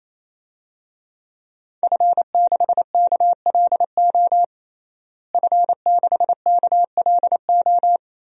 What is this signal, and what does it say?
Morse code, heard as a keyed tone, about 28 wpm, 700 Hz. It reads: F6KLO F6KLO
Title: DXCC-100 Call Signs - 5 Characters - Repeated 3x - 28wpm